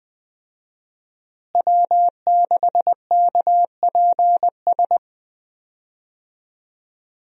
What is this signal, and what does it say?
Morse code, heard as a keyed tone, about 20 wpm, 700 Hz. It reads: W6KPS
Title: USA Call Signs - 5 Characters - Repeated 2x - 20wpm